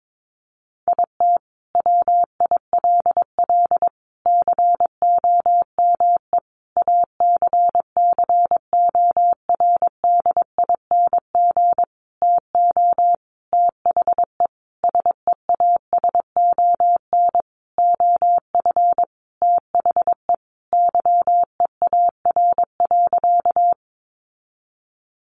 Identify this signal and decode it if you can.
Morse code, heard as a keyed tone, about 22 wpm, 700 Hz. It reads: IT WILL COME ACCORDING TO THE SEASON OF THE YEAR.
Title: Sentences from Top 1400 Words 22wpm